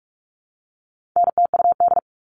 Morse code, heard as a keyed tone, about 44 wpm, 700 Hz. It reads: NTUD
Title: All Cut Numbers - 4-Digit 45wpm